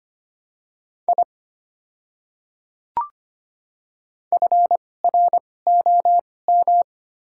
Morse code, heard as a keyed tone, about 25 wpm, spaced as for 18 wpm, 700 Hz. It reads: I FROM
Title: 2x Character Spacing - Top 300 Words - 25wpm